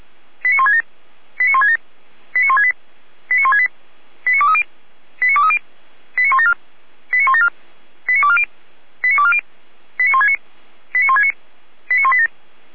Five-tone selective call (ZVEI), heard as quick runs of five tones, tones of about 70 ms. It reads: 89167 89167 89168 89168 89230 89230 89174 89174 89240 89240 89169 89169 89187